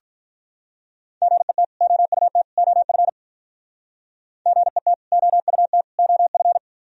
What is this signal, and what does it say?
Morse code, heard as a keyed tone, about 45 wpm, 700 Hz. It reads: GET OUT OF GET OUT OF